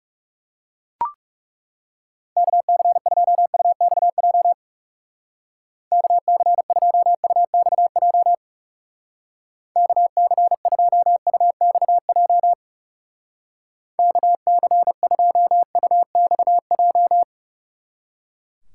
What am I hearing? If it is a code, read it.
Morse code, heard as a keyed tone, about 38 wpm, 700 Hz. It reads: KC2UXJ KC2UXJ KC2UXJ KC2UXJ